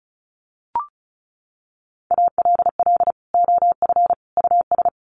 Morse code, heard as a keyed tone, about 34 wpm, 700 Hz. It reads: ALL OF US
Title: Sets of 3 Words 35wpm